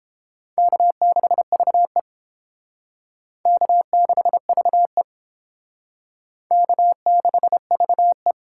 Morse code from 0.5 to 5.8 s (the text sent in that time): K6VE K6VE